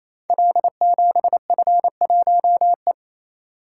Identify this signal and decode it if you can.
Morse code, heard as a keyed tone, about 29 wpm, 700 Hz. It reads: L7F1E
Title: Code Groups - 5 Letter:Numbers - 28wpm